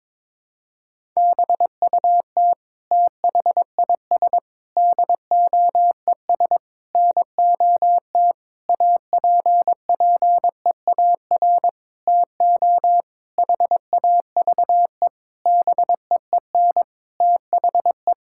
Morse code, heard as a keyed tone, about 22 wpm, 700 Hz. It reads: BUT THIS DOES NOT APPEAR TO HAVE BEEN THE
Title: Sentences from Top 1000 Words - Review Part 6 - 22wpm